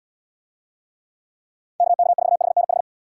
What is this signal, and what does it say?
Morse code, heard as a keyed tone, about 75 wpm, 700 Hz. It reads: KK4HNH